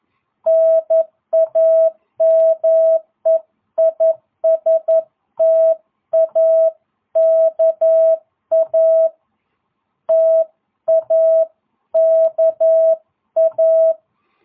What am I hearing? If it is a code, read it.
Morse code, heard as a keyed tone, about 11 wpm, 650 Hz. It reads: NAME IS TAKA TAKA